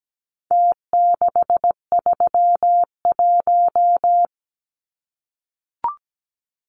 Morse code, heard as a keyed tone, about 17 wpm, 700 Hz. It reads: T631
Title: Numbers - Contest Style - 4-Digit 17wpm